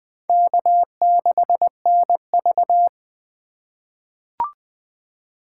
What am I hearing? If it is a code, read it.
Morse code, heard as a keyed tone, about 20 wpm, 700 Hz. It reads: K6NV
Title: USA Call Signs - 4 Characters - 2x Character Spacing - 20wpm